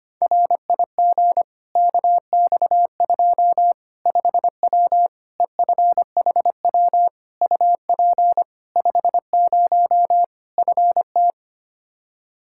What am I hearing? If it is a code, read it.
Morse code, heard as a keyed tone, about 25 wpm, 700 Hz. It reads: RIG KX2 5W EFHW UP 50 FT
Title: Practice QSO - Exchange 2 - Info - 25wpm